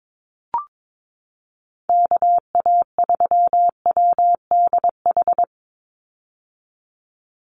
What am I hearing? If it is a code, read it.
Morse code, heard as a keyed tone, about 22 wpm, 700 Hz. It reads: KA3WDH
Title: USA Call Signs 22wpm